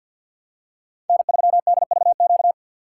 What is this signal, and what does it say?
Morse code, heard as a keyed tone, about 53 wpm, 700 Hz. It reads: N3DVQ
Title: USA Call Signs - 5 Characters - 4x Character Spacing - 50wpm